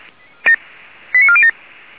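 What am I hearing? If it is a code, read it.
Five-tone selective call (ZVEI), heard as quick runs of five tones, tones of about 70 ms. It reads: 89407
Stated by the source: Received on SDR.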